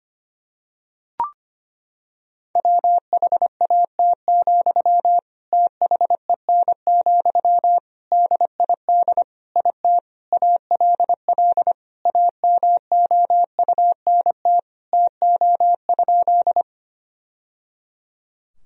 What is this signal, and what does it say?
Morse code, heard as a keyed tone, about 25 wpm, 700 Hz. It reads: WHAT, THEN, DID IT ALL AMOUNT TO?
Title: Sentences from Top 1000 Words - Review Part 4 - 25wpm